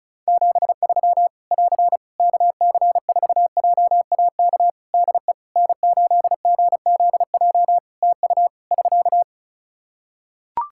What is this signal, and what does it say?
Morse code, heard as a keyed tone, about 34 wpm, 700 Hz. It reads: 73 <AR> KC4JAK DE N8GZJ TU <SK>
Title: Practice QSO - Exchange 4 - Ending - 35wpm